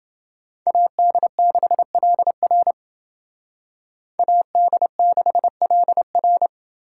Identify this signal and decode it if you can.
Morse code, heard as a keyed tone, about 28 wpm, 700 Hz. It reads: AD6LR AD6LR